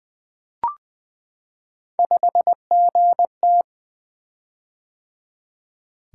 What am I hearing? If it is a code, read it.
Morse code, heard as a keyed tone, about 20 wpm, 700 Hz. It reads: 5GT
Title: Three Letter-Number Combinations - 20wpm